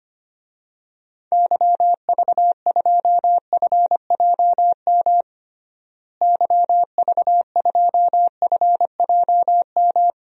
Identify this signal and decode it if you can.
Morse code, heard as a keyed tone, about 25 wpm, 700 Hz. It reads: YV2FJM YV2FJM
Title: DXCC-100 Call Signs - 6 Characters - Repeated 2x - 25wpm